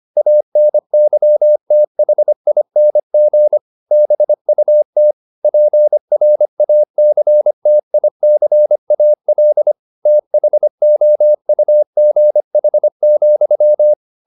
Morse code, heard as a keyed tone, about 25 wpm, 600 Hz. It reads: ANYTHING BUT PRACTICAL THOUGH,
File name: the old man and the sea part 1 25wpm